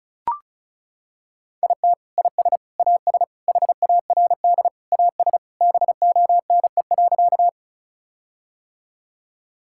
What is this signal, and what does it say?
Morse code, heard as a keyed tone, about 34 wpm, 700 Hz. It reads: IT IS AS HARD AS BONE.
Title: Sentences from Top 1600 Words 35wpm